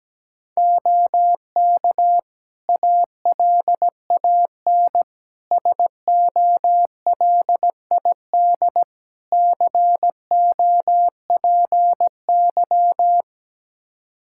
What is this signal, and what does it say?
Morse code, heard as a keyed tone, about 17 wpm, 700 Hz. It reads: OK ALAN SOLID COPY
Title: Practice QSO - Exchange 3 - More Info - 17wpm